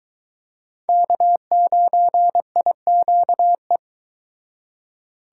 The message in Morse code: K9IQE

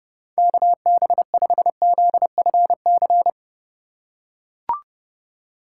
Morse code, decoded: KB5ZFC

